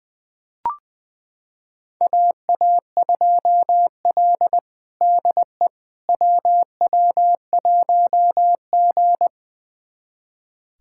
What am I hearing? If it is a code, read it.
Morse code, heard as a keyed tone, about 21 wpm, 700 Hz. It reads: AA2L DE WW1G
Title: USA Callsign DE Callsign - 4 Characters - 20wpm